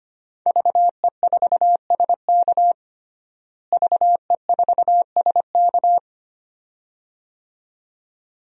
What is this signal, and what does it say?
Morse code, heard as a keyed tone, about 25 wpm, 700 Hz. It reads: VE4SK VE4SK
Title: DXCC-100 Call Signs - 5 Characters - Repeated 2x - 25wpm